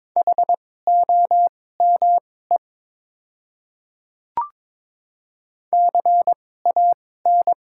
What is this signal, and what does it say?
Morse code, heard as a keyed tone, about 22 wpm, spaced as for 16 wpm, 690 Hz. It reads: HOME CAN
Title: 2x Character Spacing - Top 200 Words - 22wpm